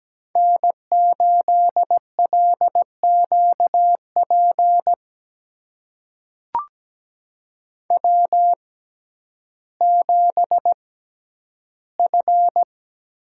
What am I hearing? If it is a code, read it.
Morse code, heard as a keyed tone, about 17 wpm, spaced as for 10 wpm, 700 Hz. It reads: N8LQP W7F
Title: USA Call Signs - 5 Characters - 6x Character Spacing - 17wpm